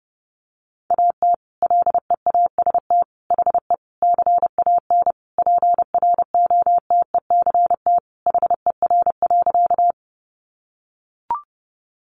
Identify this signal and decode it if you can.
Morse code, heard as a keyed tone, about 28 wpm, 700 Hz. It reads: AT LEAST HE CAN PROTECT HER.
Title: Sentences from Top 900 Words 30wpm